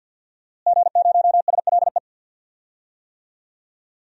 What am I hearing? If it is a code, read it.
Morse code, heard as a keyed tone, about 46 wpm, 700 Hz. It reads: G0SLE
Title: DXCC-100 Call Signs - 5 Characters - Repeated 2x - 50wpm